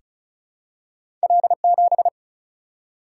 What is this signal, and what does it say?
Morse code, heard as a keyed tone, about 34 wpm, 700 Hz. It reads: L7
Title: Single-Letter Single-Number 35wpm